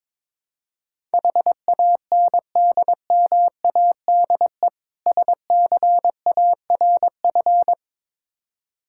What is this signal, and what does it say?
Morse code, heard as a keyed tone, about 22 wpm, 690 Hz. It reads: HANDMADE SCARF
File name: Difficult Words with Hints - 22wpm